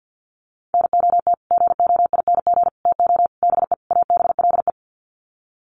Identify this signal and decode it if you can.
Morse code, heard as a keyed tone, about 48 wpm, 700 Hz. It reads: NOT GOING TO BE ABLE
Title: Sets of 5 Words 50wpm